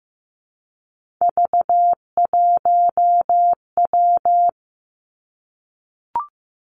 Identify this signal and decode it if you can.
Morse code, heard as a keyed tone, about 15 wpm, 700 Hz. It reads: V1W
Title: Three Challenging Letter-Number Combinations 15wpm